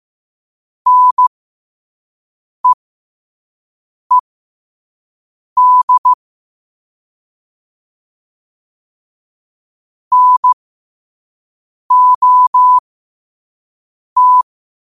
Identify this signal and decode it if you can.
Morse code, heard as a keyed tone, about 15 wpm, spaced as for 5 wpm, 1000 Hz. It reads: NEED NOT
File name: qotdfiles / 5wpm